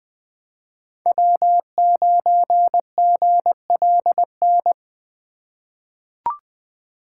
Morse code, heard as a keyed tone, about 20 wpm, 700 Hz. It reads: W9GLN
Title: USA Call Signs - 5 Characters - 3x Character Spacing - 20wpm